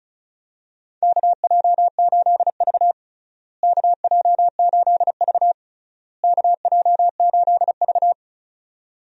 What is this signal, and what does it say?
Morse code, heard as a keyed tone, about 34 wpm, 700 Hz. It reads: KJ8V KJ8V KJ8V
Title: USA Call Signs - 4 Characters - Repeated 3x - 35wpm